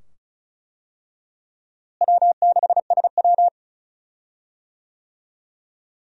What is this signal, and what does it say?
Morse code, heard as a keyed tone, about 34 wpm, 700 Hz. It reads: W6SW
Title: USA Call Signs - 4 Characters - Repeated 2x - 35wpm